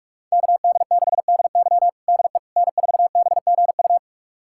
Morse code, heard as a keyed tone, about 46 wpm, 700 Hz. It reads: KD6DY DE N4BGU
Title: USA Callsign DE Callsign - 5 Characters - 45wpm